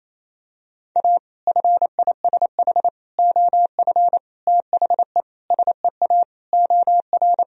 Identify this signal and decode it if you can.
Morse code, heard as a keyed tone, about 28 wpm, 700 Hz. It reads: A FISH OF THE SEA OR
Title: Sentences from Top 1000 Words - Review Part 1 - 28wpm